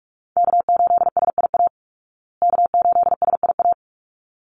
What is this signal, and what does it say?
Morse code, heard as a keyed tone, about 44 wpm, 700 Hz. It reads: K8SIA K8SIA
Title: Call Signs Repeated 3x - 45wpm